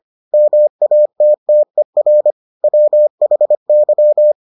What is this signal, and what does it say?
Morse code, heard as a keyed tone, about 25 wpm, 600 Hz. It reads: MATTER WHY